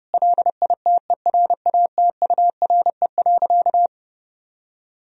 Morse code, heard as a keyed tone, about 32 wpm, 700 Hz. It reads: LITERATURE.
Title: Sentences from Top 1000 Words - Review Part 3 - 30wpm